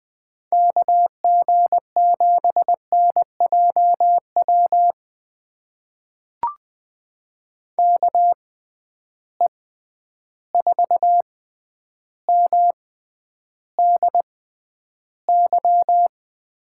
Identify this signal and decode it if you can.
Morse code, heard as a keyed tone, about 20 wpm, spaced as for 10 wpm, 700 Hz. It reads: KG7NJW KE4MDY
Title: USA Call Signs - 6 Characters - 6x Character Spacing - 20wpm